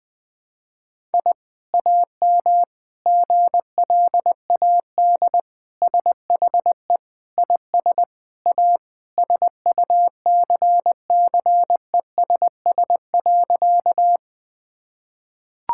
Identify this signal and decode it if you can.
Morse code, heard as a keyed tone, about 20 wpm, 700 Hz. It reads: I AM GLAD SHE IS A SUCCESS.
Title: Sentences from Top 1000 Words - Review Part 6 - 20wpm